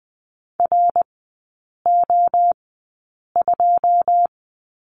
Morse code, heard as a keyed tone, about 20 wpm, spaced as for 8 wpm, 700 Hz. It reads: RO2
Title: 100 Lesson 17e - Letter u - Three Characters 20wpm